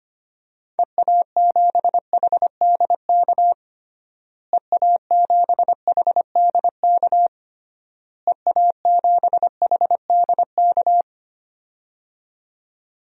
Morse code, heard as a keyed tone, about 25 wpm, 700 Hz. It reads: EA7HDK EA7HDK EA7HDK
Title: DXCC-100 Call Signs - 6 Characters - Repeated 3x - 25wpm